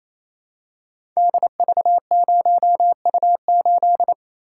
Morse code, heard as a keyed tone, about 27 wpm, 700 Hz. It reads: DV0U8
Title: Code Groups - 5 Letter:Numbers - 28wpm